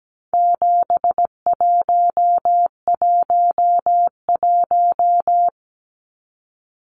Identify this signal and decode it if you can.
Morse code, heard as a keyed tone, about 17 wpm, 700 Hz. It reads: 7111
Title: Numbers - 4-Digit 17wpm